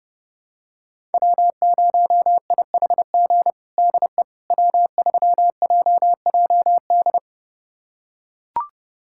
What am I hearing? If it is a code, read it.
Morse code, heard as a keyed tone, about 32 wpm, 700 Hz. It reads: W0IHG DE W3JJD
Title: USA Callsign DE Callsign - 5 Characters - 30wpm